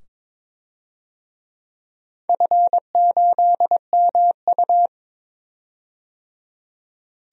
Morse code, heard as a keyed tone, about 22 wpm, 700 Hz. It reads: F8MU
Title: DXCC-100 Call Signs - 4 Characters - 22wpm